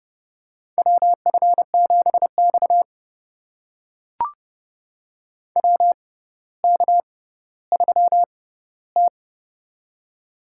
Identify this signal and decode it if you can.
Morse code, heard as a keyed tone, about 31 wpm, spaced as for 15 wpm, 700 Hz. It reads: WF7X WK3T
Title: USA Call Signs - 4 Characters - 6x Character Spacing - 30wpm